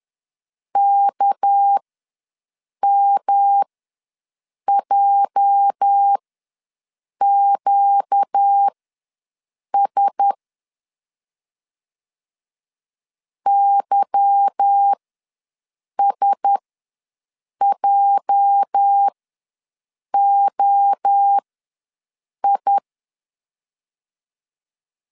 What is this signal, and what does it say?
Morse code, heard as a keyed tone, about 11 wpm, spaced as for 6 wpm, 800 Hz. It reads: KMJQS YSJOI
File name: LEZIONI / 8 YJ45 / 8 YJ ws 10wpm 4043